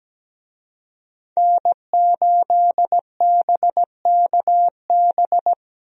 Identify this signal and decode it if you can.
Morse code, heard as a keyed tone, about 17 wpm, 700 Hz. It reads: N8BKB
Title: USA Call Signs - 5 Characters - 17wpm